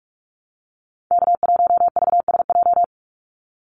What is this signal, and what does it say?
Morse code, heard as a keyed tone, about 44 wpm, 700 Hz. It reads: K1VSJ
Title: Call Signs - 45wpm